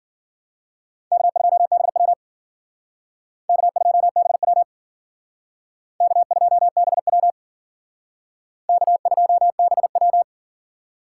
Morse code, heard as a keyed tone, about 52 wpm, 700 Hz. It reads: K2BW K2BW K2BW K2BW